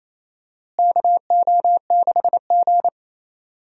Morse code, heard as a keyed tone, about 27 wpm, 700 Hz. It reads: KO6G
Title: USA Call Signs - 4 Characters - Repeated 3x - 28wpm